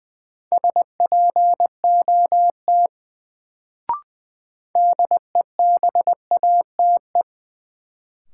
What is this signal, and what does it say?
Morse code, heard as a keyed tone, about 20 wpm, 700 Hz. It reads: SPOT DEBATE
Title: Sentences from Top 1100 Words - New Word Drill Alt - 20wpm